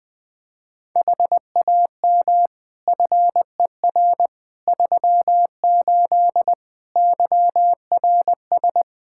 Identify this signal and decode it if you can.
Morse code, heard as a keyed tone, about 20 wpm, 700 Hz. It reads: HAM FER 38 YRS
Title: K1USN SST QSO Practice 20wpm